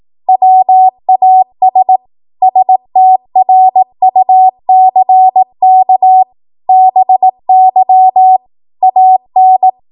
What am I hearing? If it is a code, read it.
Morse code, heard as a keyed tone, about 18 wpm, 750 Hz. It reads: WAS STRUCK BY AN
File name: Morse / 18WPM